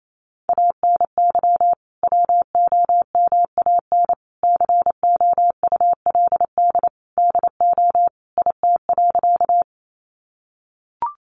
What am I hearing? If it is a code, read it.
Morse code, heard as a keyed tone, about 28 wpm, 700 Hz. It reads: ANY WOMAN COULD DO IT.
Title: Sentences from Top 1100 Words 28wpm